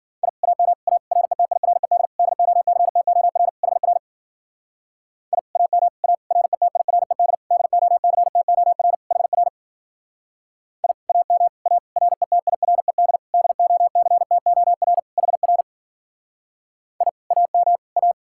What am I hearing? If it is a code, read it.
Morse code, heard as a keyed tone, about 50 wpm, 700 Hz. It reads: I AM A RETIRED DOCTOR HR I AM A RETIRED DOCTOR HR I AM A RETIRED DOCTOR HR I AM A
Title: Speed-Racing Common QSO Phrases - 40wpm